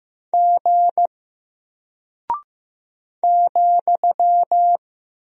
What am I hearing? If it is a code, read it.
Morse code, heard as a keyed tone, about 15 wpm, 700 Hz. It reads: G ,
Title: Single Character (Letter:Number:Punctuation:Prosign) Alt 15wpm